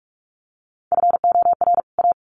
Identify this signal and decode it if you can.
Morse code, heard as a keyed tone, about 44 wpm, 700 Hz. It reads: FOR A
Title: Sets of 2 Words 45wpm